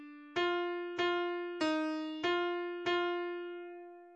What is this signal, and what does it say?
There is no code.